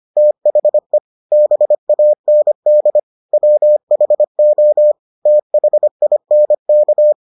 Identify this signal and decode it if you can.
Morse code, heard as a keyed tone, about 25 wpm, 600 Hz. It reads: THE BAND WHO THINK